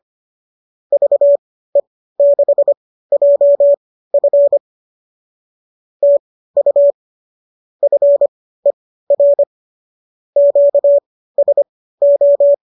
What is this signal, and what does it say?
Morse code, heard as a keyed tone, about 25 wpm, spaced as for 15 wpm, 570 Hz. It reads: VE6JF TU FER QSO
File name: qso203 15